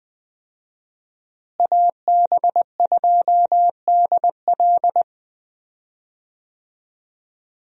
Morse code, heard as a keyed tone, about 20 wpm, 700 Hz. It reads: AB2DL
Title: USA Call Signs - 5 Characters - Repeated 2x - 20wpm